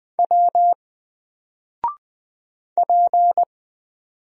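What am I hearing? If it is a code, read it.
Morse code, heard as a keyed tone, about 20 wpm, 700 Hz. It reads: W P
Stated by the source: Single letter and number Morse drill at 20 wpm, from N4LCD.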